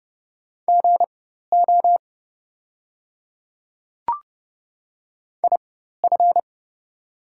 Morse code, heard as a keyed tone, about 30 wpm, spaced as for 14 wpm, 700 Hz. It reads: GO IF
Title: 4x Character Spacing - Top 100 Words - 30wpm